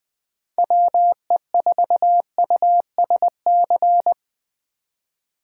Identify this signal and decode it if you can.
Morse code, heard as a keyed tone, about 24 wpm, 700 Hz. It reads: WE4USC
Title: Speed Racing - USA 2x3 Call Signs 20wpm